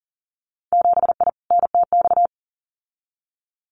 Morse code, heard as a keyed tone, about 39 wpm, 700 Hz. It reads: 7I NTX
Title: Winter Field Day Exchanges Encore 40wpm